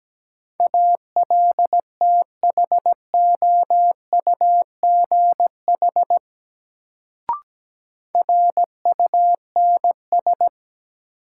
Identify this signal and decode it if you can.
Morse code, heard as a keyed tone, about 17 wpm, 700 Hz. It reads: ALTHOUGH RUNS
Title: Sentences from Top 600 Words - New Word Drill - 17wpm